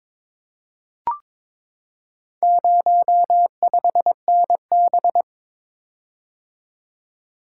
Morse code, heard as a keyed tone, about 22 wpm, 700 Hz. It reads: 05NB
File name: practice / 22wpm / Four Letter-Number Combinations - 22wpm